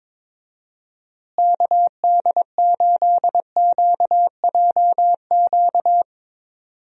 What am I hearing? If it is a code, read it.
Morse code, heard as a keyed tone, about 22 wpm, 700 Hz. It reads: KD8QJQ
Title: USA Call Signs - 6 Characters - Repeated 2x - 22wpm